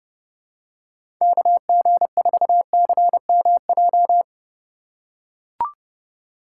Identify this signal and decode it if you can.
Morse code, heard as a keyed tone, about 30 wpm, 700 Hz. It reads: KG4CMJ
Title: USA Call Signs - 6 Characters - 3x Character Spacing - 30wpm